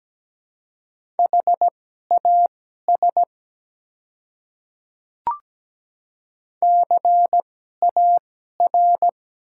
Morse code, heard as a keyed tone, about 17 wpm, spaced as for 12 wpm, 690 Hz. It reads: HAS CAR
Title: 2x Character Spacing - Top 300 Words - 17wpm